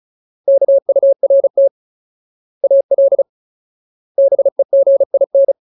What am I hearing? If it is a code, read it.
Morse code, heard as a keyed tone, about 34 wpm, 550 Hz. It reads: KURT AL BEGIN
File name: Ultimate Word List / 35WPM Ultimate